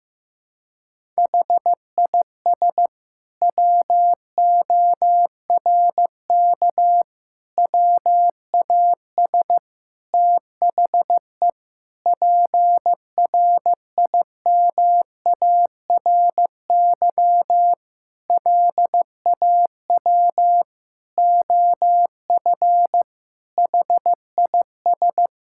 Morse code, heard as a keyed tone, about 14 wpm, 700 Hz. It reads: HIS WORK WAS THE PRIMARY LAW OF HIS
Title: Sentences from Top 1000 Words - Review Part 5 - 15wpm